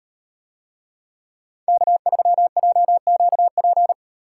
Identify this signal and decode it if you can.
Morse code, heard as a keyed tone, about 38 wpm, 700 Hz. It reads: K3JQP